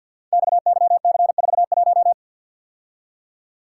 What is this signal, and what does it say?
Morse code, heard as a keyed tone, about 46 wpm, 690 Hz. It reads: XYC41
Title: Code Groups - 5 Letter:Numbers - 50wpm